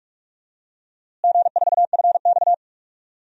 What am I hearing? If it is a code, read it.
Morse code, heard as a keyed tone, about 46 wpm, 700 Hz. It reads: G4FX
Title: DXCC-100 Call Signs - 4 Characters - Repeated 2x - 45wpm